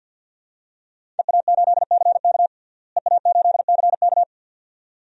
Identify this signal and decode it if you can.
Morse code, heard as a keyed tone, about 52 wpm, 700 Hz. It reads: EA8CK EA8CK
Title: DXCC-100 Call Signs - 5 Characters - Repeated 2x - 50wpm